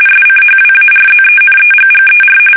RTTY, about 45.45 baud. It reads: SCHWARZENEGGER
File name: fsk850 / battleshipnj